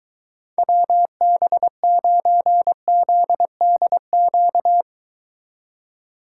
Morse code, heard as a keyed tone, about 23 wpm, 700 Hz. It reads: WB9ZDQ